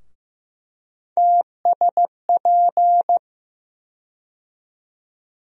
Morse code, heard as a keyed tone, about 15 wpm, 700 Hz. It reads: TSP
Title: USA Call Sign Postfixes - 3 Letters 15wpm